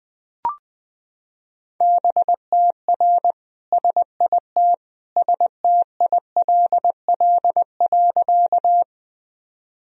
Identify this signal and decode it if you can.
Morse code, heard as a keyed tone, about 20 wpm, 700 Hz. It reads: BTR SIT STILL.
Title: Sentences from Top 500 Words and Abbreviations 20wpm